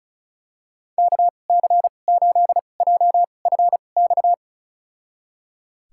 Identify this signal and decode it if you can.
Morse code, heard as a keyed tone, about 34 wpm, spaced as for 25 wpm, 700 Hz. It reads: KC8JFX
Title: USA Call Signs - 6 Characters - 2x Character Spacing - 35wpm